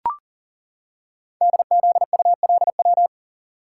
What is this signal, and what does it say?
Morse code, heard as a keyed tone, about 40 wpm, 700 Hz. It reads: DZULW
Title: Code Groups - 5 Letters - 40wpm